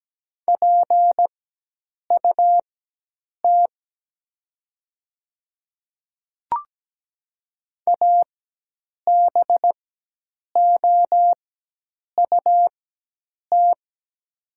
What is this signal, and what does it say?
Morse code, heard as a keyed tone, about 17 wpm, spaced as for 8 wpm, 700 Hz. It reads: PUT ABOUT
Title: 4x Character Spacing - Top 300 Words - 17wpm